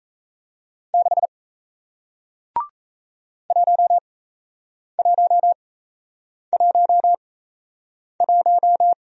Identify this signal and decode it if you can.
Morse code, heard as a keyed tone, about 37 wpm, spaced as for 8 wpm, 700 Hz. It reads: 6 1111